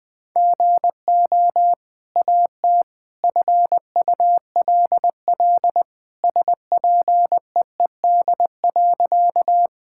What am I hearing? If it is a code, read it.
Morse code, heard as a keyed tone, about 20 wpm, 700 Hz. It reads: GO AT FULL SPEED.
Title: Sentences from Top 1000 Words - Review Part 5 - 20wpm